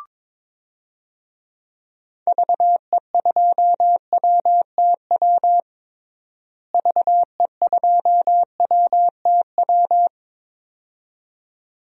Morse code, heard as a keyed tone, about 21 wpm, 700 Hz. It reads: VE2WTW VE2WTW
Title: DXCC-100 Call Signs - 6 Characters - Repeated 2x - 22wpm